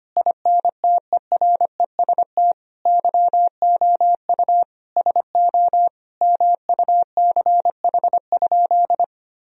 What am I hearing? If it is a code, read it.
Morse code, heard as a keyed tone, about 25 wpm, 700 Hz. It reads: INTEREST YOU SO MUCH?
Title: Sentences from Top 1000 Words - Review Part 2 - 25wpm